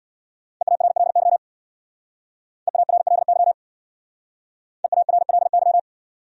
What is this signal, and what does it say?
Morse code, heard as a keyed tone, about 67 wpm, 700 Hz. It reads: EARLY EARLY EARLY